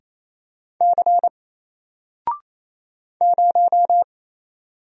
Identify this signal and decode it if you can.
Morse code, heard as a keyed tone, about 27 wpm, 700 Hz. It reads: C 0
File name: Single-Letter-Number-28wpm